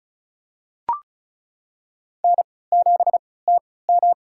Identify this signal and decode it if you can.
Morse code, heard as a keyed tone, about 34 wpm, spaced as for 20 wpm, 700 Hz. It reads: N7TM
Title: USA Call Signs - 4 Characters - 3x Character Spacing - 35wpm